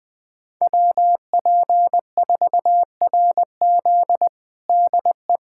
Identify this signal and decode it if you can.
Morse code, heard as a keyed tone, about 20 wpm, 700 Hz. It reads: WP4RZ DE
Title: USA Callsign DE Callsign - 5 Characters - 20wpm